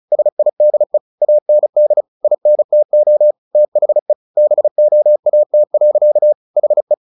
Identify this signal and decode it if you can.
Morse code, heard as a keyed tone, about 34 wpm, 600 Hz. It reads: SIDE AND INTO THE BOAT. HE
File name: the old man and the sea / the old man and the sea 1 35wpm